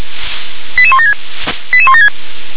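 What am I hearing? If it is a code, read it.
Five-tone selective call (ZVEI), heard as quick runs of five tones, tones of about 70 ms. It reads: 88176 88176
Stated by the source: ZVEI five-tone call received by SDR.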